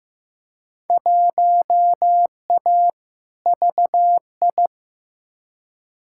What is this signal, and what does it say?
Morse code, heard as a keyed tone, about 15 wpm, 700 Hz. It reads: 1A VI
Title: Field Day Exchanges Encore - 15wpm